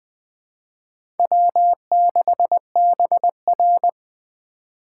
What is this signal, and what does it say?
Morse code, heard as a keyed tone, about 20 wpm, 700 Hz. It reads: W6BR
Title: USA Call Signs - 4 Characters - 3x Character Spacing - 20wpm